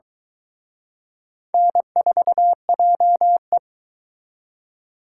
Morse code, heard as a keyed tone, about 23 wpm, 700 Hz. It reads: N4JE